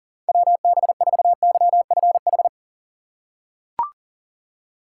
Morse code, heard as a keyed tone, about 40 wpm, 700 Hz. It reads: WB4YFH
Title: USA Call Signs - 6 Characters - 2x Character Spacing - 40wpm